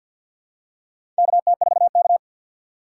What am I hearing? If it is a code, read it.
Morse code, heard as a keyed tone, about 53 wpm, 700 Hz. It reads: KT4K
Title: USA Call Signs - 4 Characters - 6x Character Spacing - 50wpm